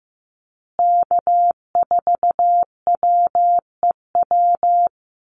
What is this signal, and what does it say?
Morse code, heard as a keyed tone, about 15 wpm, 700 Hz. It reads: K4WEW
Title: USA Call Signs Repeated 2x 15wpm